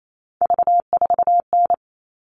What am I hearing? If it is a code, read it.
Morse code, heard as a keyed tone, about 28 wpm, 700 Hz. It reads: V4N
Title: All Cut Numbers - 3-Digit 28wpm